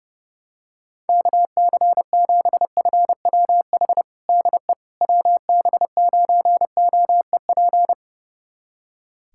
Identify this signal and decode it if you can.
Morse code, heard as a keyed tone, about 30 wpm, 700 Hz. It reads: KC7FWH DE WB9OEP
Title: USA Callsign DE Callsign - 6 Characters - 30wpm